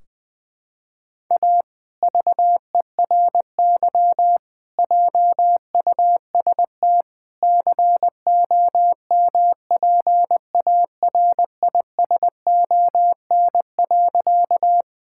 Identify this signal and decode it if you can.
Morse code, heard as a keyed tone, about 20 wpm, 700 Hz. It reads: A VERY JUST COMPARISON.
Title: Sentences from Top 1600 Words 20wpm